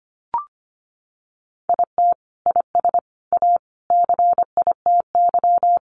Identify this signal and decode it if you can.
Morse code, heard as a keyed tone, about 25 wpm, 700 Hz. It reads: IT IS A CITY
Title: Sentences from Top 1500 Words 25wpm